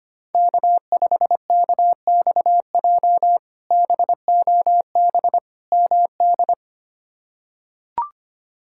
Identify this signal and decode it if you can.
Morse code, heard as a keyed tone, about 25 wpm, 700 Hz. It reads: K5KXJ BOB MD
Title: K1USN SST QSO Practice - 25wpm